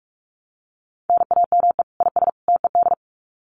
Morse code, heard as a keyed tone, about 44 wpm, 700 Hz. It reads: NAME IS TED
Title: Common QSO Phrases 45wpm